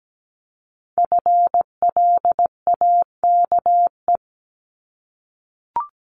Morse code, heard as a keyed tone, about 17 wpm, 700 Hz. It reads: FLAKE
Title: Multi-letter Phonemes 17wpm